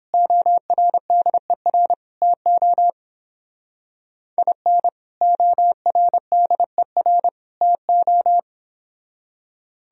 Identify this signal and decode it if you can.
Morse code, heard as a keyed tone, about 29 wpm, 700 Hz. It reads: ORDER TO IN ORDER TO